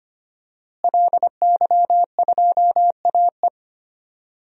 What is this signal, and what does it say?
Morse code, heard as a keyed tone, about 25 wpm, 700 Hz. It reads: LY2AE
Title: DXCC-100 Call Signs - 5 Characters - Repeated 2x - 25wpm